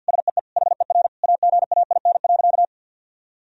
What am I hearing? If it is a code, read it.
Morse code, heard as a keyed tone, about 50 wpm, 700 Hz. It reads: SEE HER AGAIN.
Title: Sentences from Top 1000 Words - Review Part 3 - 50wpm